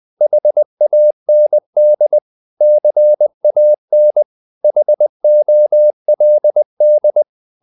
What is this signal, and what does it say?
Morse code, heard as a keyed tone, about 20 wpm, 600 Hz. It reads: HAND CAN HOLD
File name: the old man and the sea part 2 20wpm